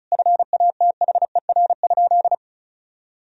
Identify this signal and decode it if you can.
Morse code, heard as a keyed tone, about 34 wpm, 700 Hz. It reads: FATHER?
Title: Sentences from Top 400 Words and Abbreviations 35wpm